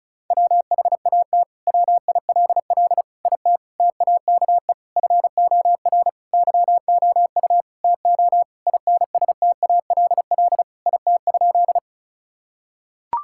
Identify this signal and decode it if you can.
Morse code, heard as a keyed tone, about 36 wpm, 700 Hz. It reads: WHAT WILL IT TAKE FOR YOU TO INSTALL IT?
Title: Sentences from Top 1600 Words 35wpm